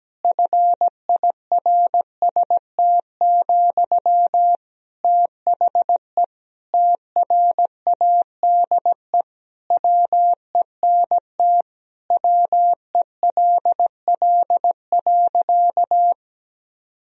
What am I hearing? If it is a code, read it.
Morse code, heard as a keyed tone, about 17 wpm, 700 Hz. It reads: FIRST, THE TRADE WENT WELL.
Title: Sentences from Top 1000 Words - Review Part 4 - 17wpm